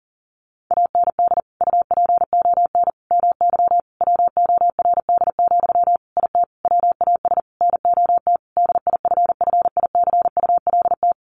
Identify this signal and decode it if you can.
Morse code, heard as a keyed tone, about 40 wpm, 700 Hz. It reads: AND UPON MY WORD, IT WAS NOT DIFFICULT
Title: Sentences from Top 1100 Words 40wpm